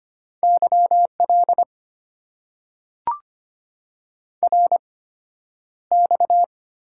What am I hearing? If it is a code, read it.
Morse code, heard as a keyed tone, about 25 wpm, spaced as for 11 wpm, 690 Hz. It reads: YL RX
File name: Top 90 QSO Elements - 8x Character Spacing - 25wpm